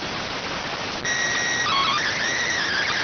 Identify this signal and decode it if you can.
SSTV (slow-scan television), PD 290.